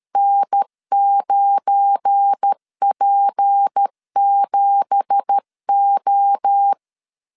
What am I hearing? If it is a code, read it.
Morse code, heard as a keyed tone, about 13 wpm, 800 Hz. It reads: N9P7O
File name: CW / DET / 1 9 1500C S C 12wpm 3811